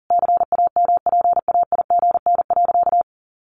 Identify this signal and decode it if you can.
Morse code, heard as a keyed tone, about 37 wpm, 700 Hz. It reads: CAMPAIGN.
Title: Sentences from Top 1200 Words 40wpm